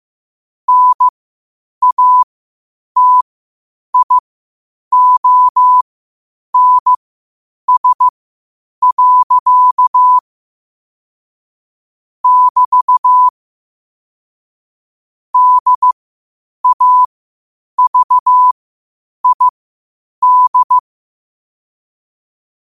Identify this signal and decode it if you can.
Morse code, heard as a keyed tone, about 15 wpm, spaced as for 8 wpm, 1000 Hz. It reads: NATIONS. <BT> DAVID
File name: qotdfiles / 8wpm